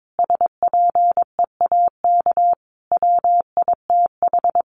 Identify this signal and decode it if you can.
Morse code, heard as a keyed tone, about 22 wpm, 700 Hz. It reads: SPEAK WITH
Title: Sentences from Top 1100 Words 22wpm